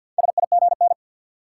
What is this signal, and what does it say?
Morse code, heard as a keyed tone, about 48 wpm, 700 Hz. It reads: SIGN
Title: Sentences from Top 1000 Words - New Word Drill - 50wpm